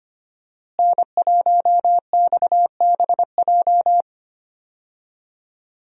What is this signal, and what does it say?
Morse code, heard as a keyed tone, about 25 wpm, 700 Hz. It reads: N1XBJ
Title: USA Call Signs - 5 Characters - Repeated 2x - 25wpm